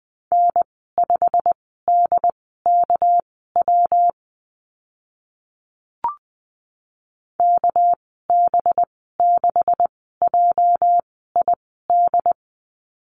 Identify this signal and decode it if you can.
Morse code, heard as a keyed tone, about 20 wpm, spaced as for 15 wpm, 700 Hz. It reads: N5DKW KB6JID
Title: 219 Lesson 35g - Number 0 - Call Signs 20wpm